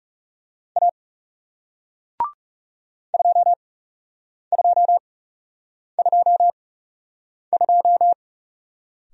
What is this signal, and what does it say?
Morse code, heard as a keyed tone, about 41 wpm, spaced as for 8 wpm, 700 Hz. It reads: A 2222